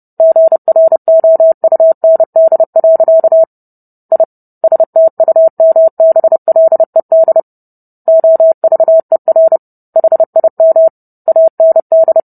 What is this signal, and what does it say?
Morse code, heard as a keyed tone, about 30 wpm, 650 Hz. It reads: GROUND. I STUMBLED OVER HIM AND